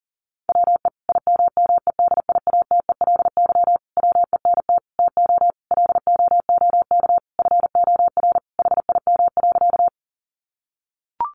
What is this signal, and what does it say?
Morse code, heard as a keyed tone, about 40 wpm, 700 Hz. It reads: WE IMMEDIATELY WENT TO LOOK FOR HIM.
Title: Sentences from Top 1100 Words 40wpm